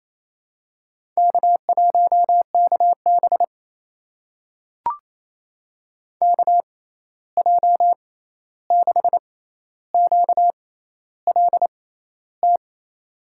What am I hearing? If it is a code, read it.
Morse code, heard as a keyed tone, about 28 wpm, spaced as for 13 wpm, 700 Hz. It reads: K1KB KJ6QLT
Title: USA Call Signs - 6x Character Spacing - 28wpm